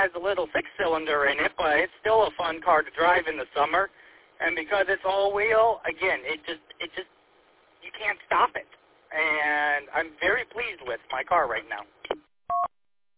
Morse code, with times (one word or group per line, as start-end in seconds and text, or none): none